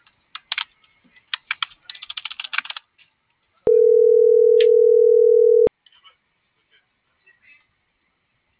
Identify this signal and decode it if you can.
Ringback tone (North America), 440 + 480 Hz.